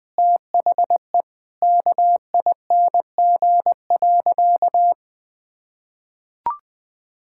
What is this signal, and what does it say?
Morse code, heard as a keyed tone, about 20 wpm, 700 Hz. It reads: THE KING.
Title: Sentences from Top 1000 Words - Review Part 6 - 20wpm